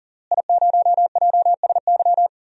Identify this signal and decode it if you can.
Morse code, heard as a keyed tone, about 37 wpm, 700 Hz. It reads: I0JSY
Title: DXCC-100 Call Signs - 5 Characters - Repeated 3x - 40wpm